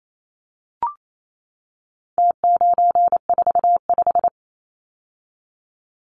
Morse code, heard as a keyed tone, about 28 wpm, 700 Hz. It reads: T945
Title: Numbers - Contest Style - 4-Digit 28wpm